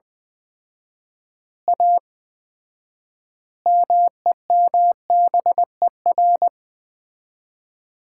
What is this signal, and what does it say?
Morse code, heard as a keyed tone, about 20 wpm, 700 Hz. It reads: A MEMBER
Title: 4x Word-Spacing - Sets of 5 Words - 20wpm